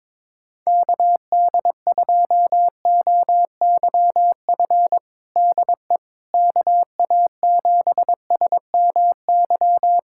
Morse code, heard as a keyed tone, about 21 wpm, 700 Hz. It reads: KD2OYF DE KA7SMY
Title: USA Callsign DE Callsign - 6 Characters - 22wpm